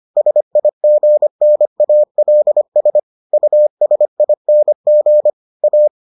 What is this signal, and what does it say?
Morse code, heard as a keyed tone, about 25 wpm, 600 Hz. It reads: SIGNALS USING A